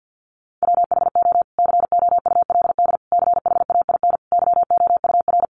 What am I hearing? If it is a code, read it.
Morse code, heard as a keyed tone, about 50 wpm, 700 Hz. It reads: WHO COULD CHAIN YOUR
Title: Sentences from Top 1500 Words 50wpm